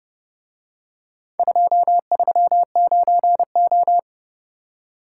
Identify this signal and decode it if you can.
Morse code, heard as a keyed tone, about 29 wpm, 700 Hz. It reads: 239O